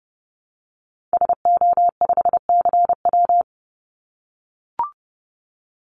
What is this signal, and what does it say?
Morse code, heard as a keyed tone, about 30 wpm, 700 Hz. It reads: SO5CW
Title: Call Signs - 30wpm